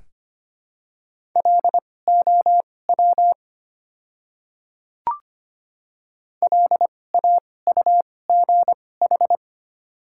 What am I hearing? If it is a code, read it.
Morse code, heard as a keyed tone, about 25 wpm, spaced as for 18 wpm, 690 Hz. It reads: LOW LAUGH
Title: 2x Character Spacing - Top 500 Words - 25wpm